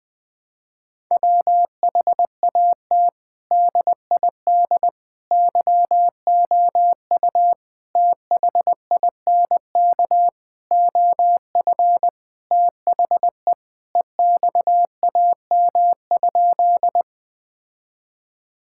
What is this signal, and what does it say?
Morse code, heard as a keyed tone, about 19 wpm, 700 Hz. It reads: WHAT DID YOU THINK OF THE EXAM?
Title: Sentences from Top 1000 Words - Review Part 5 - 20wpm